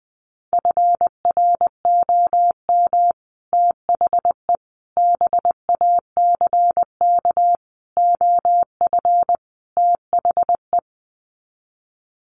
Morse code, heard as a keyed tone, about 20 wpm, 700 Hz. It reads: FROM THE BACK OF THE
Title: Sets of 5 Words Encore 20wpm